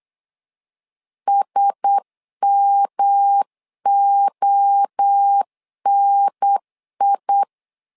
Morse code, heard as a keyed tone, about 8 wpm, 800 Hz. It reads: SMONI
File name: LEZIONI / 4 Q1S 8wpm 2710